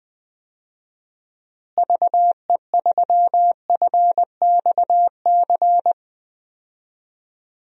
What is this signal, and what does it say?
Morse code, heard as a keyed tone, about 20 wpm, 700 Hz. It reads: VE3FXC
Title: DXCC-100 Call Signs - 6 Characters - Repeated 3x - 20wpm